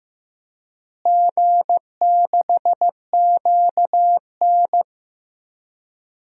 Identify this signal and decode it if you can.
Morse code, heard as a keyed tone, about 15 wpm, 700 Hz. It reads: G6QN
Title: DXCC-100 Call Signs - 4 Characters - Repeated 3x - 15wpm